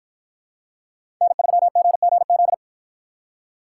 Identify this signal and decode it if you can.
Morse code, heard as a keyed tone, about 48 wpm, 700 Hz. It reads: N3GGZ